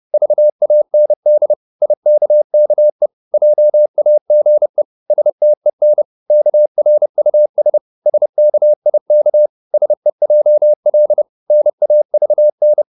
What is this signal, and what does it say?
Morse code, heard as a keyed tone, about 30 wpm, 600 Hz. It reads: VAND IKKE JAGE STEN KRUS SKIK SEJL NAVN
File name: Korte ord DK 30wpm